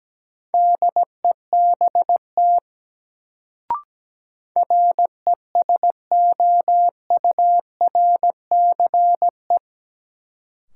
Morse code, heard as a keyed tone, about 17 wpm, 700 Hz. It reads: DEBT RESOURCE
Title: Sentences from Top 1300 Words - New Word Drill Alt - 17wpm